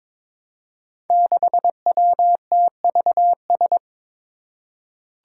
Morse code, heard as a keyed tone, about 22 wpm, 700 Hz. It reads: 6WTVS
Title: Five Challenging Letter-Number Combinations 22wpm